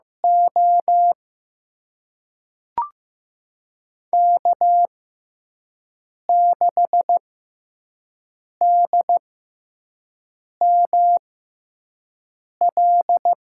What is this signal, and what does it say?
Morse code, heard as a keyed tone, about 15 wpm, spaced as for 5 wpm, 700 Hz. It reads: O K6DML
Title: USA Call Signs - 6x Character Spacing - 15wpm